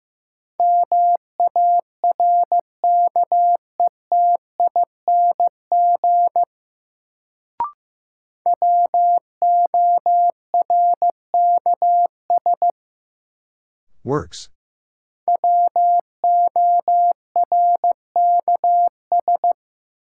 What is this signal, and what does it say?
Morse code, heard as a keyed tone, about 15 wpm, 700 Hz. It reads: MARKETING WORKS WORKS
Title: Sentences from Top 1000 Words - Review Part 5 - New Word Drill - 15wpm